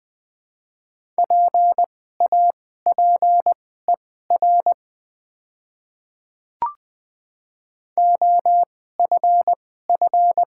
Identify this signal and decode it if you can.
Morse code, heard as a keyed tone, about 20 wpm, spaced as for 15 wpm, 700 Hz. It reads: PAPER OFF
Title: 2x Character Spacing - Top 500 Words - 20wpm